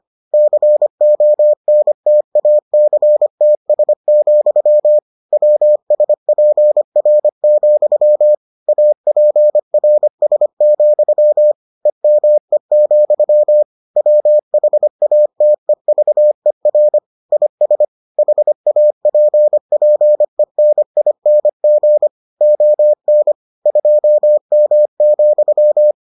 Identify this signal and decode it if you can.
Morse code, heard as a keyed tone, about 25 wpm, 600 Hz. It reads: CONTACTS, WSPR, APRS, EME, WHATEVER IS HAPPENING ON 2M,